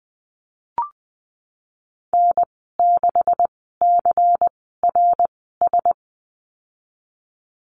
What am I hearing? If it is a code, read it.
Morse code, heard as a keyed tone, about 20 wpm, spaced as for 15 wpm, 700 Hz. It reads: N6CRS
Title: 173 Lesson 28g - Number 9 - Call Signs 20wpm